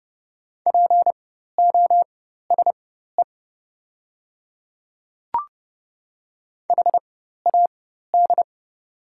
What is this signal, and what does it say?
Morse code, heard as a keyed tone, about 30 wpm, spaced as for 14 wpm, 700 Hz. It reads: POSE HAD
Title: 4x Character Spacing - Top 500 Words - 30wpm